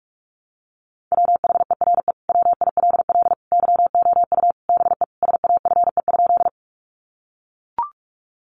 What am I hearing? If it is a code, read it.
Morse code, heard as a keyed tone, about 44 wpm, 700 Hz. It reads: WHERE WILL YOU BE SAFE?
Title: Sentences from Top 900 Words 45wpm